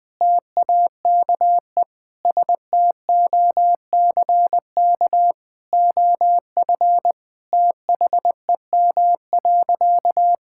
Morse code, heard as a keyed tone, about 20 wpm, 700 Hz. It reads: TAKE STOCK OF THEM.
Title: Sentences from Top 1000 Words - Review Part 5 - 20wpm